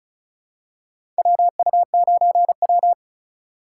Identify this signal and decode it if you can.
Morse code, heard as a keyed tone, about 36 wpm, 700 Hz. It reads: WU9W
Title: USA Call Signs - 4 Characters - 2x Character Spacing - 35wpm